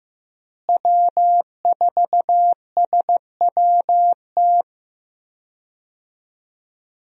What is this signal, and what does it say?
Morse code, heard as a keyed tone, about 16 wpm, 700 Hz. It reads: W4SWT
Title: USA Call Signs - 5 Characters - Repeated 2x - 15wpm